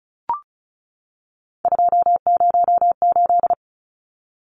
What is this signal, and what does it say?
Morse code, heard as a keyed tone, about 34 wpm, 700 Hz. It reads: 208
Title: Numbers - 3-Digit 35wpm